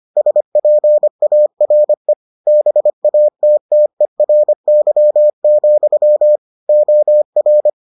Morse code, heard as a keyed tone, about 25 wpm, 600 Hz. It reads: SPARE BATTERY, OR